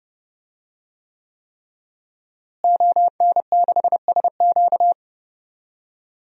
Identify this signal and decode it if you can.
Morse code, heard as a keyed tone, about 32 wpm, 700 Hz. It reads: ON6SQ
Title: DXCC-100 Call Signs - 5 Characters - Repeated 3x - 30wpm